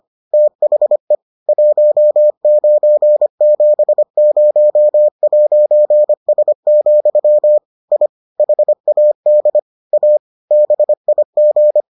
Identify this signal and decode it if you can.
Morse code, heard as a keyed tone, about 25 wpm, 600 Hz. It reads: THE 1970'S, I HAD A BIG